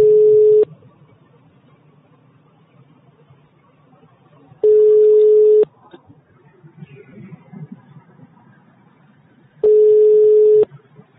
Ringback tone (Europe, 425 Hz).
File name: sauvegarde / ACRRecordings- -jusquau 17-juin-2017 / Km€ -30€ Patrik Taxizara 2 2017 05 26 19 04 02